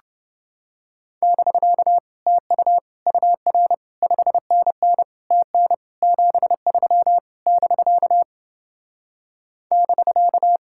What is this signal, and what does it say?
Morse code, heard as a keyed tone, about 29 wpm, 700 Hz. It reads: <BK> TU UR 5NN TN 73 <BK> <BK>